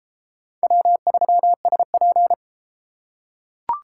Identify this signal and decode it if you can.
Morse code, heard as a keyed tone, about 29 wpm, 700 Hz. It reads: W3SP